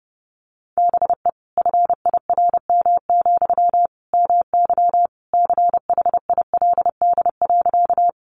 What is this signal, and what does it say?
Morse code, heard as a keyed tone, about 30 wpm, 700 Hz. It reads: BE FIRM, MY CHILD.
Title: Sentences from Top 1400 Words 30wpm